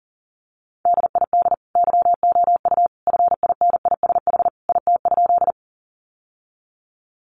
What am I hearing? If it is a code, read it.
Morse code, heard as a keyed tone, about 40 wpm, 700 Hz. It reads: DID YOU FINISH IT?
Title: Sentences from Top 1200 Words 40wpm